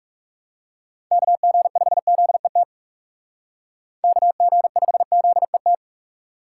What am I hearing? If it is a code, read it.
Morse code, heard as a keyed tone, about 37 wpm, 700 Hz. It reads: KG5ZET KG5ZET
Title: Speed Racing - USA 2x3 Call Signs 30wpm